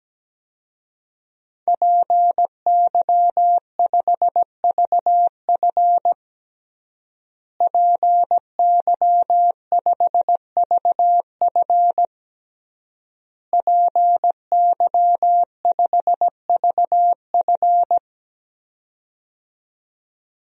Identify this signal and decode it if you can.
Morse code, heard as a keyed tone, about 17 wpm, 700 Hz. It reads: PY5VF PY5VF PY5VF
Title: DXCC-100 Call Signs - 5 Characters - Repeated 3x - 17wpm